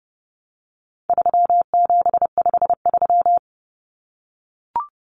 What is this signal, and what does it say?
Morse code, heard as a keyed tone, about 30 wpm, 700 Hz. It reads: 3753
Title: Common Cut Numbers - 4-Digit 30wpm